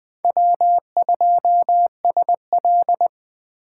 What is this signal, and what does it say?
Morse code, heard as a keyed tone, about 21 wpm, 700 Hz. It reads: W2SL
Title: Speed Racing - USA Call Signs 17wpm